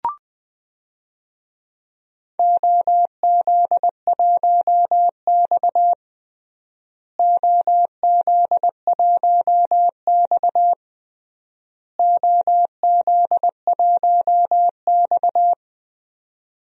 Morse code, heard as a keyed tone, about 21 wpm, 700 Hz. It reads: OZ1X OZ1X OZ1X
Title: DXCC-100 Call Signs - 4 Characters - Repeated 3x - 20wpm